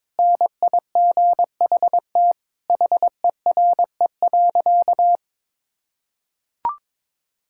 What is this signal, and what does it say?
Morse code, heard as a keyed tone, about 22 wpm, 700 Hz. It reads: NIGHT HERE.
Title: Sentences from Top 1000 Words - Review Part 5 - 22wpm